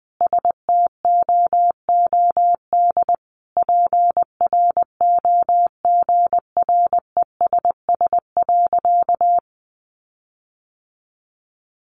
Morse code, heard as a keyed tone, about 20 wpm, 700 Hz. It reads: STOOD PROGRESS.
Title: Sentences from Top 1400 Words 20wpm